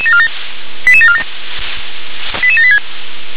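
Five-tone selective call (ZVEI), heard as quick runs of five tones, tones of about 70 ms. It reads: 88747 88786
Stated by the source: ZVEI five-tone call received by SDR.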